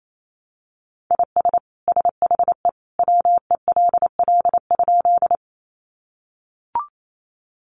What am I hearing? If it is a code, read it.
Morse code, heard as a keyed tone, about 28 wpm, 700 Hz. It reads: IS SHE WELL?
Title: Sentences from Top 200 Words 28wpm